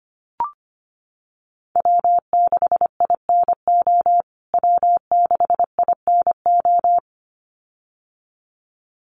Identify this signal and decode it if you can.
Morse code, heard as a keyed tone, about 25 wpm, 700 Hz. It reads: W6INO W6INO
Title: USA Call Signs Repeated 2x 25wpm